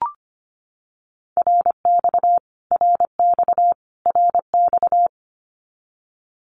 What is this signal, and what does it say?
Morse code, heard as a keyed tone, about 25 wpm, 700 Hz. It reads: RX RX RX
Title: Abbreviations Repeated 3x - 25wpm